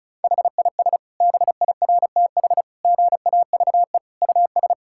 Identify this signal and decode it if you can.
Morse code, heard as a keyed tone, about 36 wpm, 700 Hz. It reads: HIS BIRTH GAVE US
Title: Sentences from Top 1000 Words - Review Part 5 - 35wpm